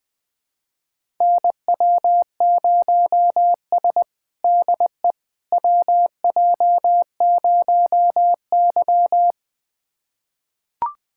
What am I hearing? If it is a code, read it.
Morse code, heard as a keyed tone, about 20 wpm, 700 Hz. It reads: NW0S DE WJ0Y
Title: USA Callsign DE Callsign - 4 Characters - 20wpm